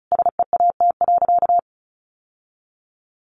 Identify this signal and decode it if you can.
Morse code, heard as a keyed tone, about 34 wpm, 700 Hz. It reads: SEAT.
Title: Sentences from Top 1500 Words 35wpm